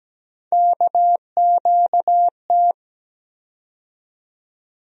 Morse code, heard as a keyed tone, about 17 wpm, 700 Hz. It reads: KQT
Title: USA Call Sign Postfixes - 3 Letters 17wpm